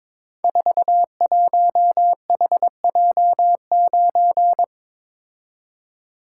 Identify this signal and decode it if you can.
Morse code, heard as a keyed tone, about 22 wpm, 700 Hz. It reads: 41HJ9
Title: Code Groups - 5 Letter:Numbers - 22wpm